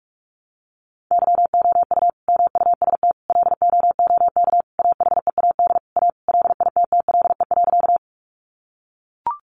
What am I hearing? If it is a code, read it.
Morse code, heard as a keyed tone, about 44 wpm, 700 Hz. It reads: YOU MUST LOOK AHEAD A LITTLE.
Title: Sentences from Top 1100 Words 45wpm